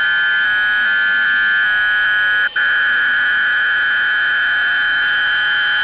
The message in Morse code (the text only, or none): none